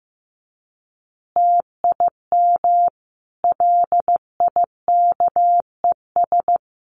Morse code, heard as a keyed tone, about 15 wpm, 700 Hz. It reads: TIM LIKES
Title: Easy Phrases 15wpm